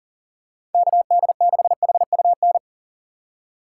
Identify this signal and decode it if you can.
Morse code, heard as a keyed tone, about 41 wpm, 700 Hz. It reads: KD6HUN
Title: USA Call Signs - 6 Characters - 4x Character Spacing - 40wpm